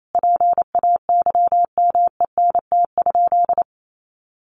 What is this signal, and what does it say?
Morse code, heard as a keyed tone, about 28 wpm, 700 Hz. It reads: PAYMENT?
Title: Sentences from Top 1400 Words 28wpm